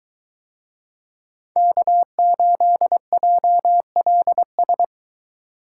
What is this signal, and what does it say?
Morse code, heard as a keyed tone, about 21 wpm, 700 Hz. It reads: K8JLS